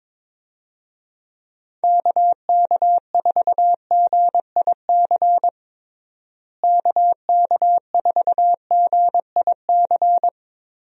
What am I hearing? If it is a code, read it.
Morse code, heard as a keyed tone, about 22 wpm, 700 Hz. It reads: KK4GIC KK4GIC
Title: USA Call Signs - 6 Characters - Repeated 2x - 22wpm